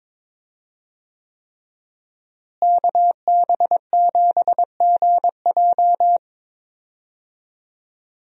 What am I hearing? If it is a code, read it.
Morse code, heard as a keyed tone, about 22 wpm, 700 Hz. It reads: KB7GJ
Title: USA Call Signs - 5 Characters - Repeated 3x - 22wpm